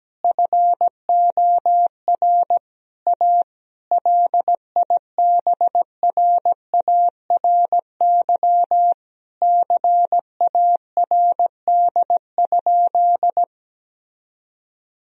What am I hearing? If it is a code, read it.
Morse code, heard as a keyed tone, about 17 wpm, 700 Hz. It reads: FOR A LIBRARY CARD?
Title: Sentences from Top 1000 Words - Review Part 4 - 17wpm